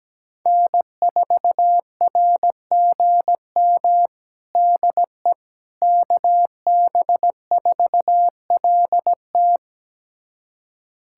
Morse code, heard as a keyed tone, about 17 wpm, 700 Hz. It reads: N4RGM DE KB4LT
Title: USA Callsign DE Callsign - 5 Characters - 17wpm